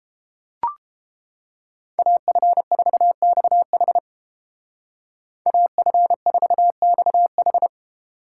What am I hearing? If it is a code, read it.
Morse code, heard as a keyed tone, about 29 wpm, 700 Hz. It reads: AF4XH AF4XH